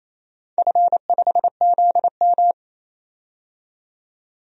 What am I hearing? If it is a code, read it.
Morse code, heard as a keyed tone, about 27 wpm, 700 Hz. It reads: F5ZM
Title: DXCC-100 Call Signs - 4 Characters - 28wpm